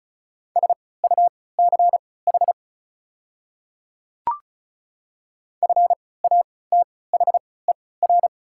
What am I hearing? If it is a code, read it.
Morse code, heard as a keyed tone, about 36 wpm, spaced as for 20 wpm, 700 Hz. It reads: SUCH FATHER
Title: 3x Character Spacing - Top 200 Words - 35wpm